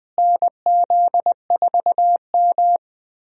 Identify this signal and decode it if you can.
Morse code, heard as a keyed tone, about 20 wpm, 700 Hz. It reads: NZ4M
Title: USA Call Signs - 4 Characters - Repeated 2x - 20wpm